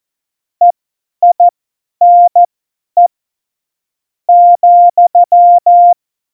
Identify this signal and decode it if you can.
Morse code, heard as a keyed tone, about 14 wpm, spaced as for 8 wpm, 700 Hz. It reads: EINE,
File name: Morsekurs / Witze / Witz 04. September 2018, 50 F70